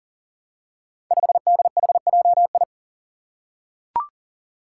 Morse code, heard as a keyed tone, about 40 wpm, 700 Hz. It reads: 5DHJI
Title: Five Challenging Letter-Number Combinations 40wpm